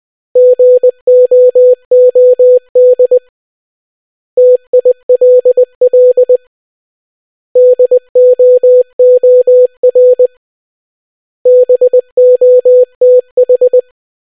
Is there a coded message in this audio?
Yes: Morse code from 0.4 to 13.8 s, heard as a keyed tone, about 20 wpm, 500 Hz. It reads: GOOD TILL DOOR BOTH